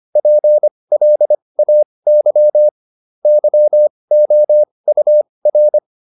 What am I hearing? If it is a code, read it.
Morse code, heard as a keyed tone, about 25 wpm, spaced as for 20 wpm, 600 Hz. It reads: PLAY YOUR